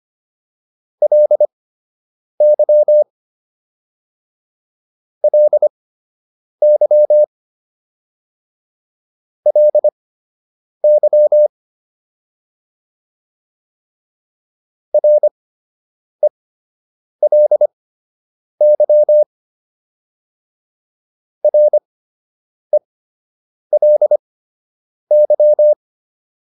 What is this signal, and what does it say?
Morse code, heard as a keyed tone, about 25 wpm, spaced as for 8 wpm, 600 Hz. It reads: LY LY LY RELY RELY